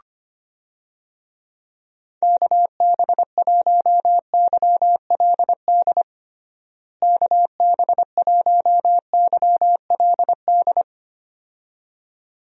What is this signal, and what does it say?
Morse code, heard as a keyed tone, about 25 wpm, 700 Hz. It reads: KB1YLD KB1YLD
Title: USA Call Signs - 6 Characters - Repeated 2x - 25wpm